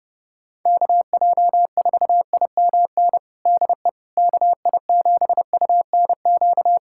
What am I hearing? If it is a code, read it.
Morse code, heard as a keyed tone, about 31 wpm, 700 Hz. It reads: KJ4IMN DE KI7UNQ
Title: USA Callsign DE Callsign - 6 Characters - 30wpm